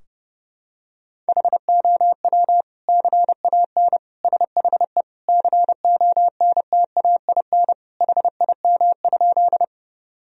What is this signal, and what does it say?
Morse code, heard as a keyed tone, about 30 wpm, 700 Hz. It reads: HOW CAN SHE CONTAIN HIM?
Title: Sentences from Top 1000 Words - Review Part 2 - 30wpm